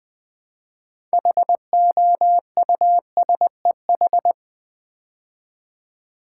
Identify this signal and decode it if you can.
Morse code, heard as a keyed tone, about 20 wpm, 700 Hz. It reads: HOUSEH
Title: Build A Word - 3 Letters At A Time - 9-Letter Words - 20wpm